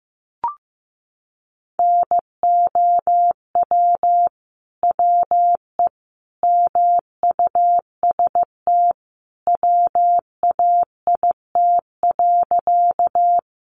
Morse code, heard as a keyed tone, about 15 wpm, 700 Hz. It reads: NOW WE MUST WAIT.
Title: Sentences from Top 500 Words 15wpm